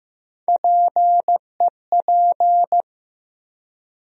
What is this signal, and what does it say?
Morse code, heard as a keyed tone, about 15 wpm, 700 Hz. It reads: PEP
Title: USA Call Sign Postfixes - 3 Letters 15wpm